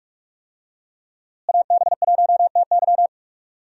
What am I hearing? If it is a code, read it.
Morse code, heard as a keyed tone, about 47 wpm, 700 Hz. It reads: AB1TY
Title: USA Call Signs - 5 Characters - Repeated 2x - 45wpm